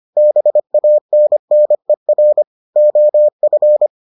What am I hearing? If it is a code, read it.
Morse code, heard as a keyed tone, about 25 wpm, 600 Hz. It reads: BANNER OF